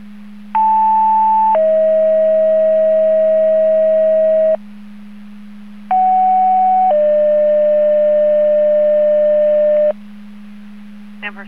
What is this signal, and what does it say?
Two-tone page, heard as a one-second tone, then a three-second tone, about 880 then 640 Hz, about 750 then 600 Hz. It is heard from 0.5 to 9.9 s.